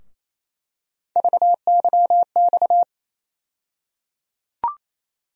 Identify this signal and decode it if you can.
Morse code, heard as a keyed tone, about 28 wpm, 700 Hz. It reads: VYX
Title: USA Call Sign Postfixes - 3 Letters 28wpm